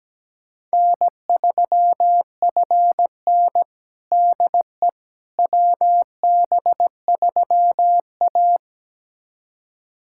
Morse code, heard as a keyed tone, about 17 wpm, 700 Hz. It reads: N3FN DE WB3A
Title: USA Callsign DE Callsign - 4 Characters - 17wpm